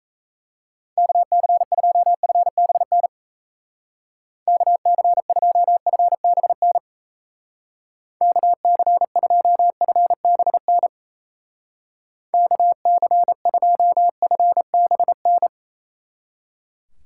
Morse code, heard as a keyed tone, about 34 wpm, 700 Hz. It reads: KC2FBN KC2FBN KC2FBN KC2FBN
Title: Speed Racing - USA 2x3 Call Signs 28wpm